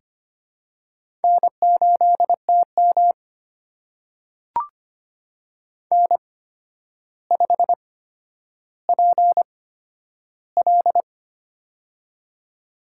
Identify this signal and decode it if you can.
Morse code, heard as a keyed tone, about 25 wpm, spaced as for 11 wpm, 700 Hz. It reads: N8TM N5PL
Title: USA Call Signs - 8x Character Spacing - 25wpm